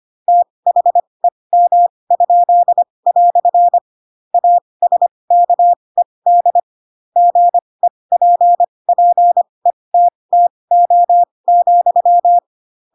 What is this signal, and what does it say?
Morse code, heard as a keyed tone, about 25 wpm, spaced as for 20 wpm, 700 Hz. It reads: THEM?" ASKED GEPPETTO,
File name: Pinocchio / Pinocchio ch7 15 25 ik1hge